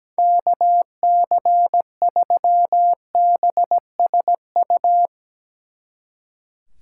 Morse code, heard as a keyed tone, about 18 wpm, 700 Hz. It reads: KC3BSU
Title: Speed Racing - USA 2x3 Call Signs 17wpm